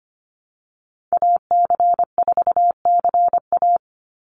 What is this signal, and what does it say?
Morse code, heard as a keyed tone, about 25 wpm, 700 Hz. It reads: AC4CA
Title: Call Signs Repeated 3x - 25wpm